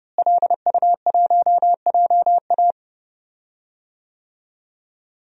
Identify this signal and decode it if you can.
Morse code, heard as a keyed tone, about 31 wpm, 700 Hz. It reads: LU1JA
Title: DXCC-100 Call Signs - 5 Characters - Repeated 3x - 30wpm